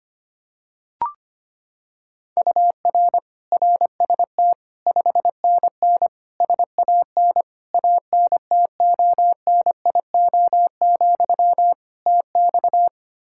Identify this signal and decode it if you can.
Morse code, heard as a keyed tone, about 25 wpm, 700 Hz. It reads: UR RST 5NN SAN ANTONIO, TX
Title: Practice QSO - Exchange 1 - Beginning - 25wpm